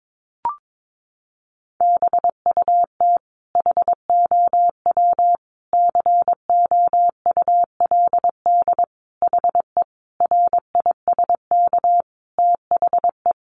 Morse code, heard as a keyed tone, about 22 wpm, 700 Hz. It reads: BUT HOW COULD HE RISK THE
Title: Sentences from Top 600 Words 22wpm